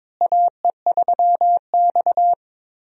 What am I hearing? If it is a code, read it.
Morse code, heard as a keyed tone, about 22 wpm, 700 Hz. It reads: AE3X
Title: USA Call Signs - 4 Characters - Repeated 2x - 22wpm